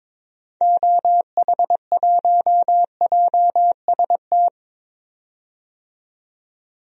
Morse code, heard as a keyed tone, about 22 wpm, 700 Hz. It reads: OH1JST
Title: DXCC-100 Call Signs - 6 Characters - 22wpm